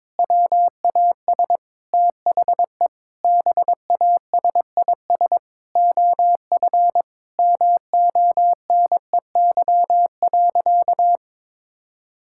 Morse code, heard as a keyed tone, about 22 wpm, 700 Hz. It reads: WAS THE BASIS OF MONEY.
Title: Sentences from Top 1000 Words - Review Part 5 - 22wpm